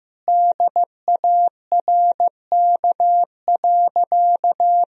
Morse code, heard as a keyed tone, about 15 wpm, 700 Hz. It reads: DARK.
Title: Sentences from Top 500 Words and Abbreviations 15wpm